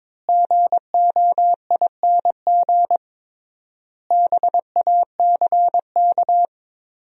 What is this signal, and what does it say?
Morse code, heard as a keyed tone, about 22 wpm, 700 Hz. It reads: GOING BACK
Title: 3x Word-Spacing - Sentences from Top 500 Words - 22wpm